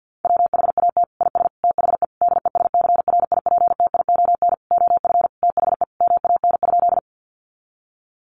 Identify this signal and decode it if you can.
Morse code, heard as a keyed tone, about 50 wpm, 700 Hz. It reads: WHAT IS THE DESCRIPTION OF THE MAN?
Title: Sentences from Top 1300 Words 50wpm